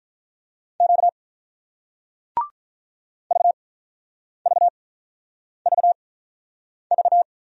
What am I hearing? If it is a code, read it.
Morse code, heard as a keyed tone, about 48 wpm, spaced as for 9 wpm, 700 Hz. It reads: Q VVVV